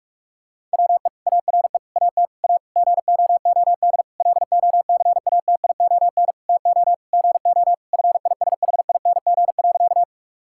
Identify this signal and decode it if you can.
Morse code, heard as a keyed tone, about 46 wpm, 700 Hz. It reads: WE ARE AT A GOOD LOCATION TO GO FISHING.
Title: Sentences from Top 1000 Words - Review Part 6 - 45wpm